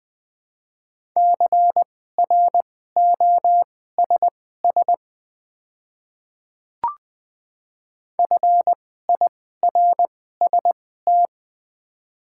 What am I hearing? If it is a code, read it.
Morse code, heard as a keyed tone, about 20 wpm, spaced as for 15 wpm, 700 Hz. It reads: CROSS FIRST
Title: 2x Character Spacing - Top 500 Words - 20wpm